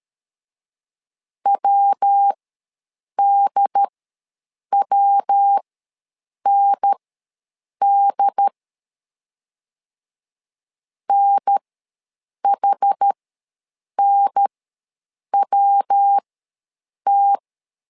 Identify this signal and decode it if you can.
Morse code, heard as a keyed tone, about 13 wpm, spaced as for 7 wpm, 800 Hz. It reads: WDWND NHNWT
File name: LEZIONI / 3 MWRO / 3 MW ws 12wpm 1816